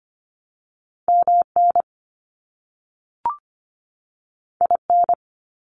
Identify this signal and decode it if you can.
Morse code, heard as a keyed tone, about 25 wpm, 700 Hz. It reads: MN IN
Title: US States 25wpm